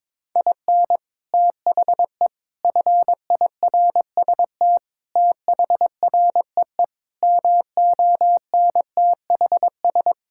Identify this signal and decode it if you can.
Morse code, heard as a keyed tone, about 22 wpm, 700 Hz. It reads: IN THE FIRST THREE MONTHS
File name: practice / 22wpm / Sets of 8 Words - 22wpm